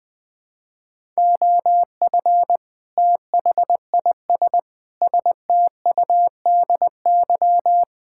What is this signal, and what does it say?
Morse code, heard as a keyed tone, about 20 wpm, 700 Hz. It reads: OF THIS STUDY
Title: Build A Phrase - 5-Word Phrases - 20wpm